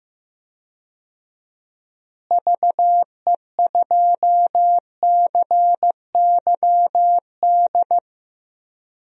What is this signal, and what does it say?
Morse code, heard as a keyed tone, about 15 wpm, 700 Hz. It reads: VE2CYD
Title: DXCC-100 Call Signs - 6 Characters - Repeated 2x - 15wpm